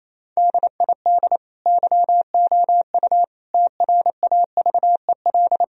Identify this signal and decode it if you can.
Morse code, heard as a keyed tone, about 28 wpm, 700 Hz. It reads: DID YOU TRAVEL
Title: Sentences from Top 1000 Words - Review Part 2 - 28wpm